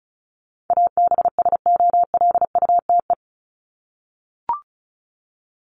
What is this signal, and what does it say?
Morse code, heard as a keyed tone, about 34 wpm, 700 Hz. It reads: ABSOLUTE
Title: LY Suffix - Top 100 Words with Base Words 35wpm